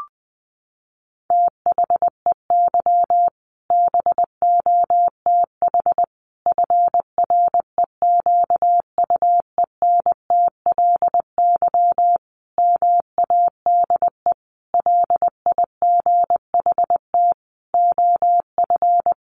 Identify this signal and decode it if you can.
Morse code, heard as a keyed tone, about 20 wpm, 700 Hz. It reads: THEY BOTH FREQUENTLY MADE LIGHT OF
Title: Sentences from Top 1300 Words 20wpm